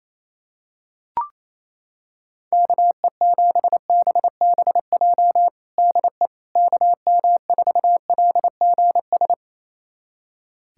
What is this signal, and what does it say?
Morse code, heard as a keyed tone, about 28 wpm, 700 Hz. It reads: KE7BBJ DE KM4LGS
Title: USA Callsign DE Callsign 28wpm